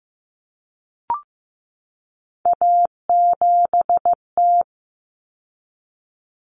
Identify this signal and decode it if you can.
Morse code, heard as a keyed tone, about 15 wpm, 700 Hz. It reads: A7T
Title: Common Cut Numbers - 3-Digit 15wpm